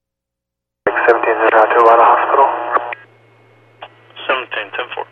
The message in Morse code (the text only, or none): none